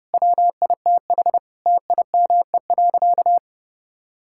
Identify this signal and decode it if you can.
Morse code, heard as a keyed tone, about 32 wpm, 700 Hz. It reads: WITH TIME.
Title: Sentences from Top 1000 Words - Review Part 5 - 30wpm